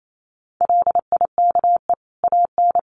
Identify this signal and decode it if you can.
Morse code, heard as a keyed tone, about 28 wpm, 700 Hz. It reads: LIKE AN
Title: Sentences from Top 1500 Words 28wpm